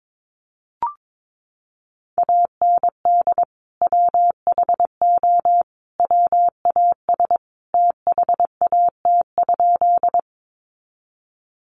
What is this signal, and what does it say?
Morse code, heard as a keyed tone, about 22 wpm, 700 Hz. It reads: AND WHO WAS THAT?
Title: Sentences from Top 100 Words 22wpm